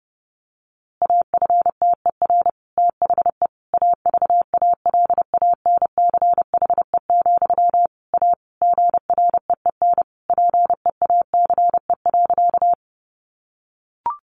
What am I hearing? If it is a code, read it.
Morse code, heard as a keyed tone, about 30 wpm, 700 Hz. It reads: AFTER THE AVALANCHE, A GREEN PEACE.
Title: A Sound Of Thunder 30wpm